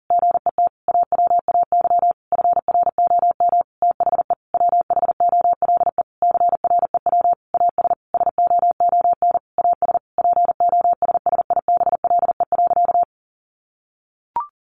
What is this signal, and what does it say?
Morse code, heard as a keyed tone, about 40 wpm, 700 Hz. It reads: GET AWAY FROM THE WHOLE CREW AS SOON AS POSSIBLE.
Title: Sentences from Top 1500 Words 40wpm